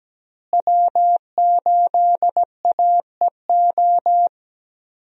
Morse code, heard as a keyed tone, about 17 wpm, 700 Hz. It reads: W8AEO
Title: USA Call Signs - 5 Characters - Repeated 2x - 17wpm